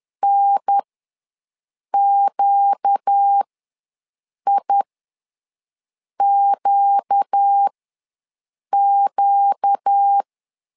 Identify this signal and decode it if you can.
Morse code, heard as a keyed tone, about 10 wpm, spaced as for 6 wpm, 800 Hz. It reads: NQIQQ
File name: LEZIONI / 4 Q ws 10wpm 2306